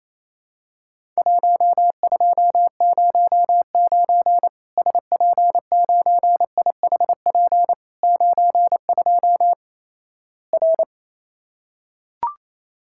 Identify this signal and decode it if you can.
Morse code, heard as a keyed tone, about 28 wpm, 700 Hz. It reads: 1209 SP9IHP 92 R
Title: Worked All Europe DX Contest - QTC Exchange - 28wpm